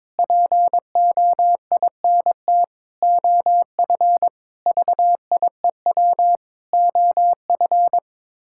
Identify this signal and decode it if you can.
Morse code, heard as a keyed tone, about 22 wpm, 700 Hz. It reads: POINT OF VIEW OF
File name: practice / 22wpm / Sets of 6 Words - 22wpm